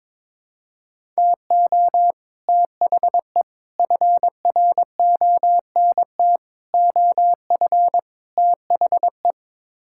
Morse code, heard as a keyed tone, about 22 wpm, 700 Hz. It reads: TO THE FRONT OF THE
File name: practice / 22wpm / Build A Phrase - 5-Word Phrases - 22wpm